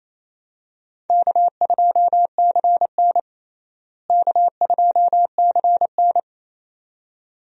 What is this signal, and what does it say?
Morse code, heard as a keyed tone, about 28 wpm, 700 Hz. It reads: K2CN K2CN
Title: USA Call Signs - 4 Characters - Repeated 2x - 28wpm